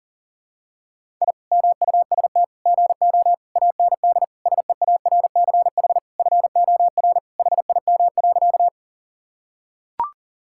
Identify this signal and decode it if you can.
Morse code, heard as a keyed tone, about 40 wpm, 700 Hz. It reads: I MUST GO AND SEARCH FOR HIM.
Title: Sentences from Top 1000 Words - Review Part 5 - 40wpm